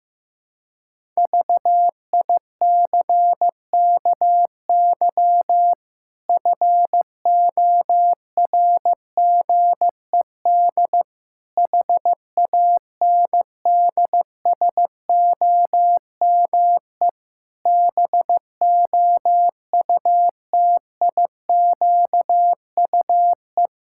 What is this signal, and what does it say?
Morse code, heard as a keyed tone, about 15 wpm, 700 Hz. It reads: VICKY FORGED HANDSOME BOUTIQUE
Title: Pangram Challenge v2 - 15wpm